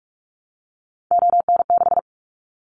Morse code, heard as a keyed tone, about 46 wpm, 700 Hz. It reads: ON6
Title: Callsign Prefixes for Top 25 DXCC - Repeated 3x - 45wpm